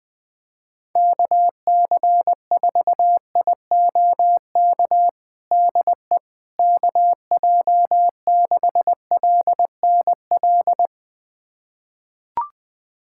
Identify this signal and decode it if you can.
Morse code, heard as a keyed tone, about 20 wpm, 700 Hz. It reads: KC4IOK DE KJ6LNL
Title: USA Callsign DE Callsign - 6 Characters - 20wpm